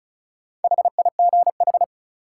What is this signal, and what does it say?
Morse code, heard as a keyed tone, about 36 wpm, 700 Hz. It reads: HIGH
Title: Sentences from Top 200 Words - New Word Drill - 35wpm